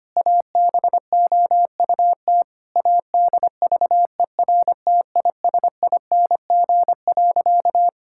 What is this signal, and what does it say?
Morse code, heard as a keyed tone, about 25 wpm, 700 Hz. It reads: ABOUT ADVERTISING.
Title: Sentences from Top 1000 Words - Review Part 6 - 25wpm